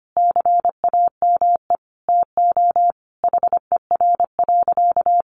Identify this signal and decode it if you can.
Morse code, heard as a keyed tone, about 25 wpm, 700 Hz. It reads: CAME TO HER.
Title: Sentences from Top 1100 Words 25wpm